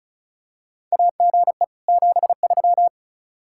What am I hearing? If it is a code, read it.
Morse code, heard as a keyed tone, about 36 wpm, 700 Hz. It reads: AGE 73
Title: Practice QSO - Exchange 3 - More Info - 35wpm